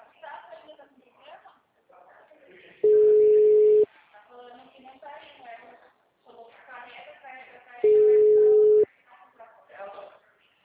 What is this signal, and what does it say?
Ringback tone (Europe, 425 Hz), with speech.
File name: out- -2014- - - .47663